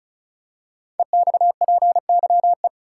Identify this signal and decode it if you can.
Morse code, heard as a keyed tone, about 36 wpm, 690 Hz. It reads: EXPYE
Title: Code Groups - 5 Letters - 35wpm